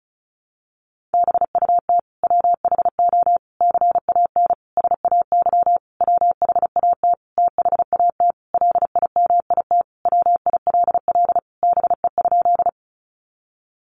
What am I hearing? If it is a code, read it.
Morse code, heard as a keyed tone, about 34 wpm, 700 Hz. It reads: BUT WHO CAN SAY WHAT THAT LIMIT WILL BE?
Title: Sentences from Top 1500 Words 35wpm